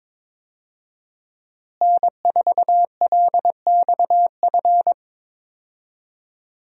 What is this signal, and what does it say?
Morse code, heard as a keyed tone, about 22 wpm, 700 Hz. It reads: N4LXF
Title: USA Call Signs - 5 Characters - Repeated 2x - 22wpm